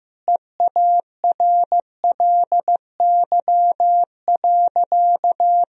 Morse code, heard as a keyed tone, about 16 wpm, 700 Hz. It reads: EARLY.
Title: Sentences from Top 1000 Words - Review Part 4 - 15wpm